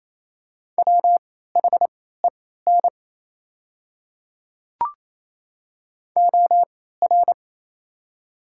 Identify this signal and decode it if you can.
Morse code, heard as a keyed tone, about 28 wpm, spaced as for 16 wpm, 700 Hz. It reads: WHEN OR
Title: 3x Character Spacing - Top 500 Words - 28wpm